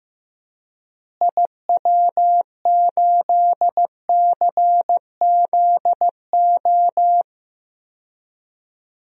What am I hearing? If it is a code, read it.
Morse code, heard as a keyed tone, about 15 wpm, 700 Hz. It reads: IW8CZO
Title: DXCC-100 Call Signs - 6 Characters - Repeated 2x - 15wpm